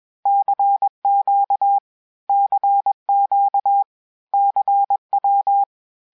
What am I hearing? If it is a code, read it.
Morse code, heard as a keyed tone, about 21 wpm, 800 Hz. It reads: CQ CQ CW
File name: cours-du-lundi files / cours-1580-abreviations-mpm-20